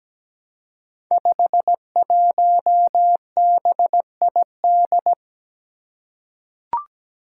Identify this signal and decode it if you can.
Morse code, heard as a keyed tone, about 17 wpm, 700 Hz. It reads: 51BID
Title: Five Challenging Letter-Number Combinations 17wpm